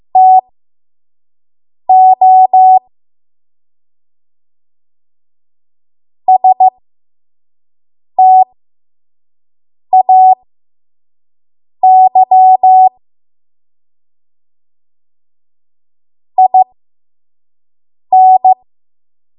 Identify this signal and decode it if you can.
Morse code, heard as a keyed tone, about 15 wpm, spaced as for 5 wpm, 750 Hz. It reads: TO STAY IN
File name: Archive / 5 WPM / 05WPM